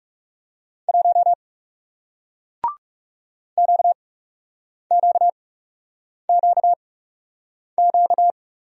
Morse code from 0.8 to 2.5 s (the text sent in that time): 1